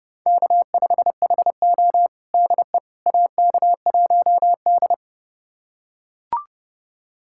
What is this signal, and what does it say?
Morse code, heard as a keyed tone, about 30 wpm, 700 Hz. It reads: K5HO DE AK1D
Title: USA Callsign DE Callsign - 4 Characters - 30wpm